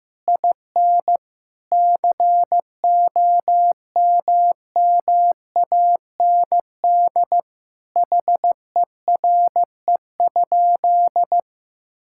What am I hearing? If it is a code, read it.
Morse code, heard as a keyed tone, about 15 wpm, 700 Hz. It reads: IN COMMAND HERE?
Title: Sentences from Top 1600 Words 15wpm